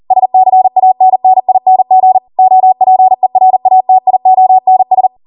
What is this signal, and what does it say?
Morse code, heard as a keyed tone, about 40 wpm, 750 Hz. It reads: SCANNING OPERATIONS
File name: Morse / 40WPM